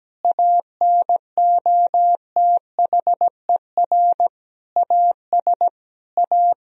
Morse code, heard as a keyed tone, about 17 wpm, 700 Hz. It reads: ANOTHER AS A
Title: Sentences from Top 1000 Words - Review Part 1 - 17wpm